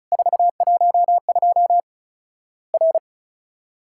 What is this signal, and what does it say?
Morse code, heard as a keyed tone, about 36 wpm, 700 Hz. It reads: 412 R
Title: Worked All Europe DX Contest - QTC Exchange - 35wpm